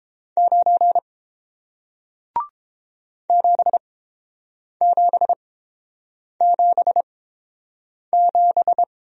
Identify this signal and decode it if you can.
Morse code, heard as a keyed tone, about 30 wpm, spaced as for 7 wpm, 700 Hz. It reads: 9 7777